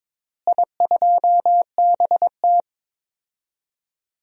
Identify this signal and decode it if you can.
Morse code, heard as a keyed tone, about 22 wpm, 700 Hz. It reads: I2BT
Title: DXCC-100 Call Signs - 4 Characters - Repeated 2x - 22wpm